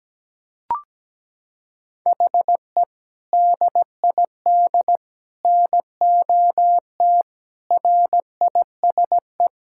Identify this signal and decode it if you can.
Morse code, heard as a keyed tone, about 17 wpm, 700 Hz. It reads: HE DID NOT RISE
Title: Sentences from Top 1000 Words - Review Part 6 - 17wpm